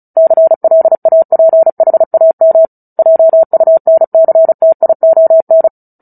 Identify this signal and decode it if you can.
Morse code, heard as a keyed tone, about 34 wpm, 650 Hz. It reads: CLAPHAM JUNCTION
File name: Book Two / 09-Chapter 9 - 35 WPM